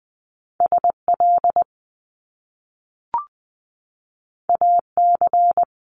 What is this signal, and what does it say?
Morse code, heard as a keyed tone, about 20 wpm, 700 Hz. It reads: SL AC
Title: Top Two Letter Combinations in Words 20wpm